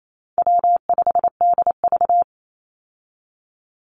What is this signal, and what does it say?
Morse code, heard as a keyed tone, about 28 wpm, 700 Hz. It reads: W5DV
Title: Call Signs Repeated 3x - 28wpm